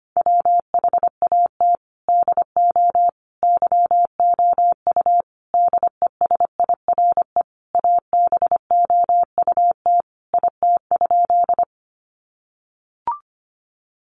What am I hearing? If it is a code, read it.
Morse code, heard as a keyed tone, about 25 wpm, 700 Hz. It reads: WHAT DO YOU DESIRE ABOUT IT?
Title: Sentences from Top 1100 Words 25wpm